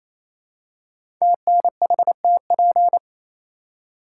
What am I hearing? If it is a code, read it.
Morse code, heard as a keyed tone, about 28 wpm, 700 Hz. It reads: TNHTP
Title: Code Groups - 5 Letters - 28wpm